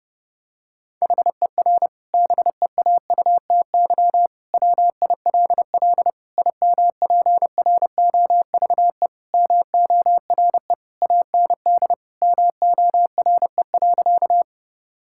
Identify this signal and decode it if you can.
Morse code, heard as a keyed tone, about 30 wpm, 700 Hz. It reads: HER BEAUTY WILL IMPROVE MORE AND MORE.
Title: Sentences from Top 1000 Words - Review Part 5 - 30wpm